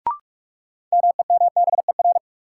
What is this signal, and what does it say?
Morse code, heard as a keyed tone, about 42 wpm, 700 Hz. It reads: MEMBER
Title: Sentences from Top 1100 Words - New Word Drill - 45wpm